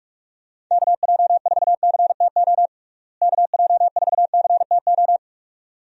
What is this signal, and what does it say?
Morse code, heard as a keyed tone, about 46 wpm, 700 Hz. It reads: KJ4CTO KJ4CTO
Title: USA Call Signs - 6 Characters - Repeated 2x - 45wpm